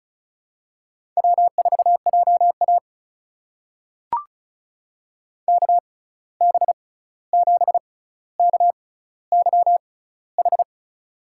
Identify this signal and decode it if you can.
Morse code, heard as a keyed tone, about 36 wpm, spaced as for 16 wpm, 700 Hz. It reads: W4JA KB7KYH
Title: USA Call Signs - 6x Character Spacing - 35wpm